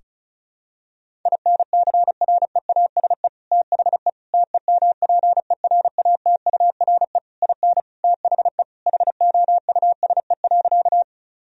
Morse code, heard as a keyed tone, about 36 wpm, 700 Hz. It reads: INCREASE THE TEMPERATURE IN THE HOUSE.
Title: Sentences from Top 1000 Words - Review Part 4 - 35wpm